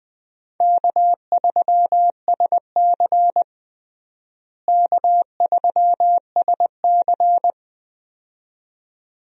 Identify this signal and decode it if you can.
Morse code, heard as a keyed tone, about 20 wpm, 700 Hz. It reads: K3SC K3SC
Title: USA Call Signs - 4 Characters - Repeated 2x - 20wpm